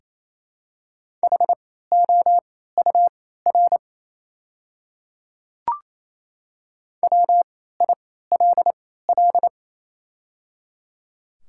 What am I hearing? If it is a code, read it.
Morse code, heard as a keyed tone, about 28 wpm, spaced as for 16 wpm, 690 Hz. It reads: HOUR WILL
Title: 3x Character Spacing - Top 500 Words - 28wpm